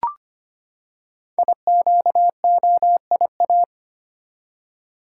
Morse code, heard as a keyed tone, about 25 wpm, 700 Hz. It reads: IQOIA
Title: Code Groups - 5 Letters - 25wpm